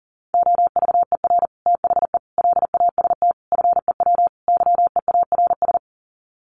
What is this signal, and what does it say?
Morse code, heard as a keyed tone, about 40 wpm, 700 Hz. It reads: OVER THE LAST FEW YEARS
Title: Sets of 5 Words Encore 40wpm